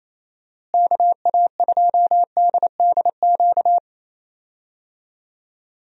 Morse code, heard as a keyed tone, about 28 wpm, 700 Hz. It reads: KA2DDQ
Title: USA Call Signs - 6 Characters - Repeated 3x - 28wpm